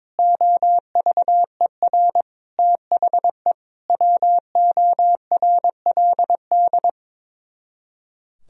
Morse code, heard as a keyed tone, about 22 wpm, 700 Hz. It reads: OVER THE WORLD
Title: Speed-Racing - Sets of 4 Words 22wpm